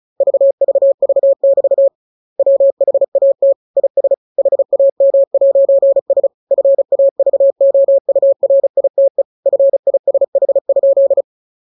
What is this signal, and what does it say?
Morse code, heard as a keyed tone, about 34 wpm, 550 Hz. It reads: VVV<BT> WHAT IS HAM'S FAVOURITE FISH?
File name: jokes / joke 6 35 0000